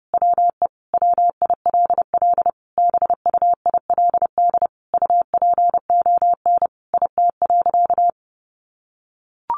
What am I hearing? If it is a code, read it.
Morse code, heard as a keyed tone, about 30 wpm, 700 Hz. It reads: WE WILL BUILD UPON IT.
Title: Sentences from Top 1400 Words 30wpm